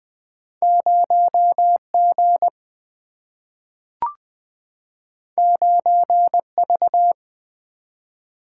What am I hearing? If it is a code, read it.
Morse code, heard as a keyed tone, about 21 wpm, 700 Hz. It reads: 0 G 9 V
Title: Single-Letter Single-Number 20wpm